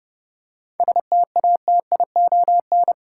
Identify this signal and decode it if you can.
Morse code, heard as a keyed tone, about 30 wpm, 700 Hz. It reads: STATION
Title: Seven Letter Words - 30wpm